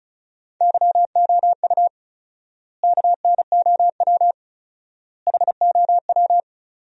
Morse code, heard as a keyed tone, about 34 wpm, 700 Hz. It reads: YOU KNOW HOW
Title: 4x Word-Spacing - Sentences from Top 100 Words - 35wpm